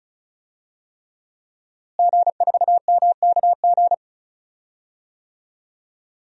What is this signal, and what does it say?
Morse code, heard as a keyed tone, about 36 wpm, 700 Hz. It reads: G4MKG
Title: DXCC-100 Call Signs - 5 Characters - Repeated 3x - 35wpm